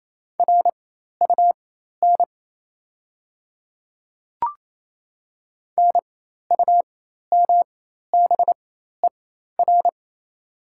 Morse code, heard as a keyed tone, about 28 wpm, spaced as for 13 wpm, 700 Hz. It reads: RUN NUMBER
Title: 4x Character Spacing - Top 300 Words - 28wpm